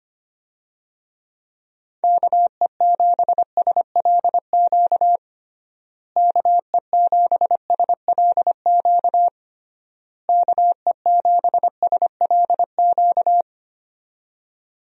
Morse code, heard as a keyed tone, about 25 wpm, 700 Hz. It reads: KE7SLQ KE7SLQ KE7SLQ
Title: USA Call Signs - 6 Characters - Repeated 3x - 25wpm